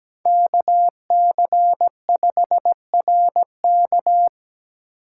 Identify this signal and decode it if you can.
Morse code, heard as a keyed tone, about 17 wpm, 700 Hz. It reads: KC5RK
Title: USA Call Signs - 4x Character Spacing - 17wpm